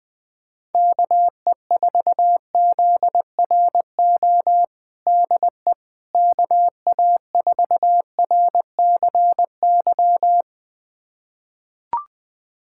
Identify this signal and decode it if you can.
Morse code, heard as a keyed tone, about 20 wpm, 700 Hz. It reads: KE4ZRO DE KA4RCY
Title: USA Callsign DE Callsign 20wpm